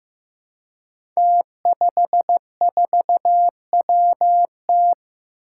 Morse code, heard as a keyed tone, about 15 wpm, 700 Hz. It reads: T54WT
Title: Five Challenging Letter-Number Combinations 15wpm